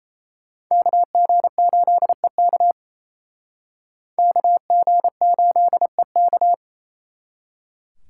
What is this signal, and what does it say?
Morse code, heard as a keyed tone, about 36 wpm, 700 Hz. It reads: KG8EK KG8EK